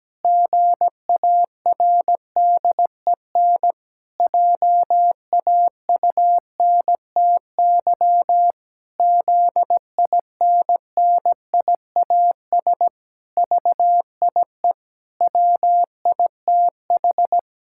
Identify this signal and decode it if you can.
Morse code, heard as a keyed tone, about 17 wpm, 700 Hz. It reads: GARDEN JAUNTY ZINNIAS VIE WITH
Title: Pangram Challenge v2 - 17wpm